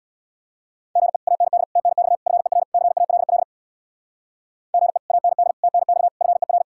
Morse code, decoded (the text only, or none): GE AND TNX FER CALL GE AND TNX FER